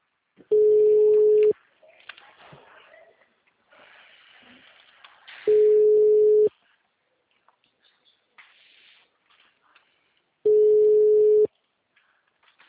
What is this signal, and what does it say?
Ringback tone (Europe, 425 Hz).